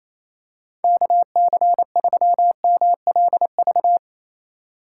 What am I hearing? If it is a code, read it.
Morse code, heard as a keyed tone, about 27 wpm, 700 Hz. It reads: KC3MLV
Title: USA Call Signs - 6 Characters - 3x Character Spacing - 28wpm